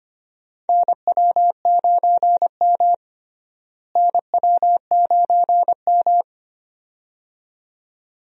Morse code, heard as a keyed tone, about 25 wpm, 700 Hz. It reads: NW9M NW9M
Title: USA Call Signs - 4 Characters - Repeated 2x - 25wpm